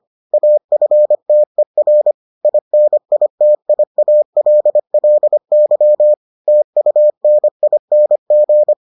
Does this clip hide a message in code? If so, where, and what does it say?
Yes: Morse code from 0.3 to 8.7 s, heard as a keyed tone, about 25 wpm, 600 Hz. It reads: AFTER INITIALLY TUNING